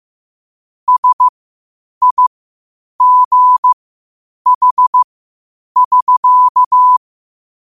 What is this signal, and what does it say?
Morse code, heard as a keyed tone, about 15 wpm, spaced as for 8 wpm, 1000 Hz. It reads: SIGH <SK>
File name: qotdfiles / 8wpm